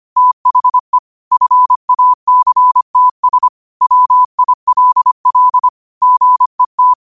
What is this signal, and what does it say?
Morse code, heard as a keyed tone, about 25 wpm, 1000 Hz. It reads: THE FACTS WILL GET
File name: qotdfiles / 25wpm